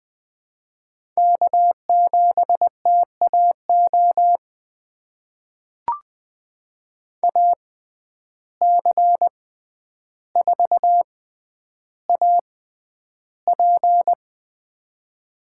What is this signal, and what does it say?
Morse code, heard as a keyed tone, about 20 wpm, spaced as for 10 wpm, 700 Hz. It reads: K7TAO AC4AP
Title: USA Call Signs - 5 Characters - 6x Character Spacing - 20wpm